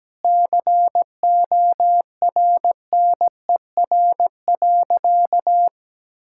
Morse code, heard as a keyed tone, about 17 wpm, 700 Hz. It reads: CORNER.
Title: Sentences from Top 1600 Words 17wpm